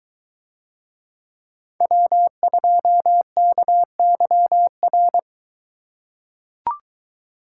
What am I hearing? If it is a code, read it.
Morse code, heard as a keyed tone, about 20 wpm, 700 Hz. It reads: W2KYR